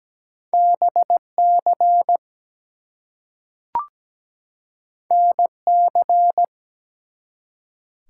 Morse code, heard as a keyed tone, about 17 wpm, 700 Hz. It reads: BC NC
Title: US States and Canadian Provinces 17wpm